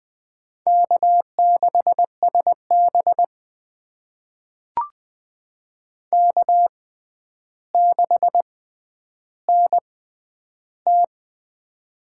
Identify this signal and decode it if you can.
Morse code, heard as a keyed tone, about 20 wpm, spaced as for 10 wpm, 700 Hz. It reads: K6SB K6NT
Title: USA Call Signs - 6x Character Spacing - 20wpm